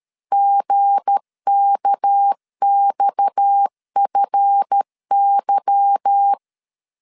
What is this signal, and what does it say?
Morse code, heard as a keyed tone, about 13 wpm, 800 Hz. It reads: GKXFY